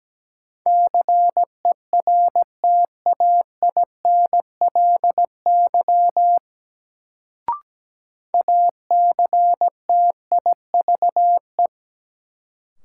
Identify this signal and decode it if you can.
Morse code, heard as a keyed tone, about 17 wpm, 700 Hz. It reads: CERTAINLY ACTIVE
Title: Sentences from Top 900 Words - New Word Drill - 17wpm